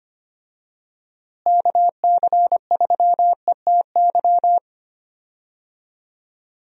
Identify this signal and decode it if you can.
Morse code, heard as a keyed tone, about 25 wpm, 700 Hz. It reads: KC3ETY
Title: USA Call Signs - 6 Characters - Repeated 2x - 25wpm